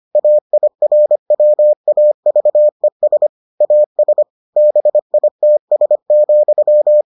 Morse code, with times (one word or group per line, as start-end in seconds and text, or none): 0.1-3.3 AIRWAVES
3.6-4.2 AS
4.6-7.0 BITS,